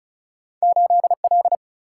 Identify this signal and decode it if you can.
Morse code, heard as a keyed tone, about 36 wpm, 700 Hz. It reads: 8L
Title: Single-Letter Single-Number 35wpm